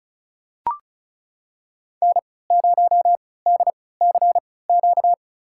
Morse code, heard as a keyed tone, about 36 wpm, spaced as for 20 wpm, 690 Hz. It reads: N0DCQ
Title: USA Call Signs - 3x Character Spacing - 35wpm